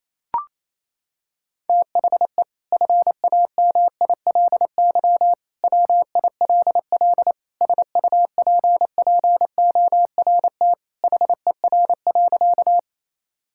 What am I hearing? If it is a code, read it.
Morse code, heard as a keyed tone, about 28 wpm, 700 Hz. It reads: THE FAMILY WILL SUPPORT HER.
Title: Sentences from Top 1000 Words - Review Part 5 - 28wpm